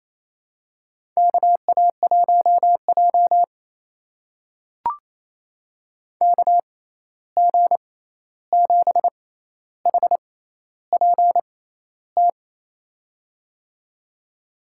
Morse code, heard as a keyed tone, about 28 wpm, spaced as for 13 wpm, 700 Hz. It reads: KA1J KG7HPT
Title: USA Call Signs - 6x Character Spacing - 28wpm